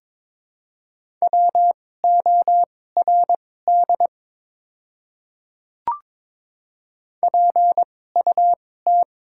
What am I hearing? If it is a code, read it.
Morse code, heard as a keyed tone, about 21 wpm, spaced as for 16 wpm, 700 Hz. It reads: WORD PUT
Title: 2x Character Spacing - Top 500 Words - 22wpm